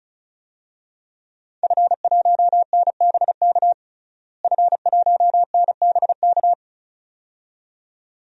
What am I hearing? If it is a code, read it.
Morse code, heard as a keyed tone, about 34 wpm, 700 Hz. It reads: F1NBK F1NBK
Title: DXCC-100 Call Signs - 5 Characters - Repeated 2x - 35wpm